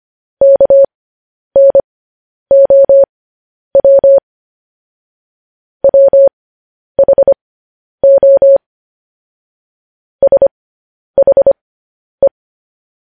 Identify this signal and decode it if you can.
Morse code, heard as a keyed tone, about 25 wpm, spaced as for 10 wpm, 550 Hz. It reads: KNOW WHO SHE
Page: news headlines in morse code at 10 wpm